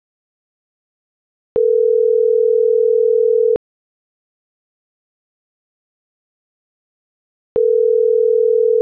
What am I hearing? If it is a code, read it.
Ringback tone (North America), 440 + 480 Hz.